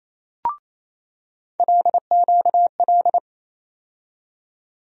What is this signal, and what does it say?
Morse code, heard as a keyed tone, about 28 wpm, 700 Hz. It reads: LQL
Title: USA Call Sign Postfixes - 3 Letters 28wpm